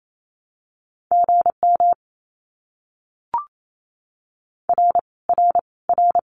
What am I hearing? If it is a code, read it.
Morse code, heard as a keyed tone, about 28 wpm, 700 Hz. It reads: GM R R R
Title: Abbreviations Repeated 3x - 28wpm